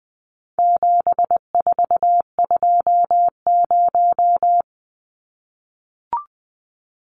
Morse code, heard as a keyed tone, about 20 wpm, 700 Hz. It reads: 7420
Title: Numbers - 4-Digit 20wpm